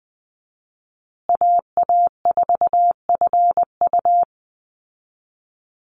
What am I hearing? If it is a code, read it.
Morse code, heard as a keyed tone, about 20 wpm, 700 Hz. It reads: AA4FU
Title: Call Signs Repeated 2x - 20wpm